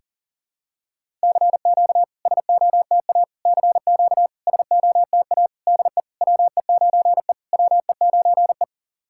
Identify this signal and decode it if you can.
Morse code, heard as a keyed tone, about 40 wpm, 700 Hz. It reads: CQ SOTA CQ SOTA DE WE9E WE9E
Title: SOTA QSO Practice 40wpm